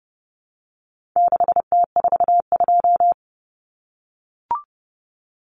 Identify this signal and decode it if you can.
Morse code, heard as a keyed tone, about 30 wpm, 700 Hz. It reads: 6T42
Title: Common Cut Numbers - 4-Digit 30wpm